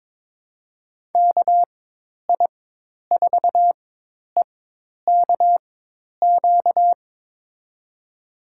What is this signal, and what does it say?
Morse code, heard as a keyed tone, about 22 wpm, spaced as for 10 wpm, 700 Hz. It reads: KI4EKQ
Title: USA Call Signs - 6 Characters - 4x Character Spacing - 22wpm